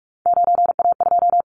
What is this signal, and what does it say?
Morse code, heard as a keyed tone, about 46 wpm, 700 Hz. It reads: 9A2
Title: Callsign Prefixes for Top 25 DXCC - 45wpm